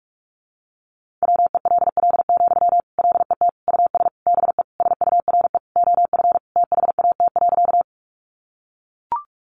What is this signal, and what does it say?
Morse code, heard as a keyed tone, about 44 wpm, 700 Hz. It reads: WELL, LET US BE SURE OF THAT.
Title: Sentences from Top 300 Words 45wpm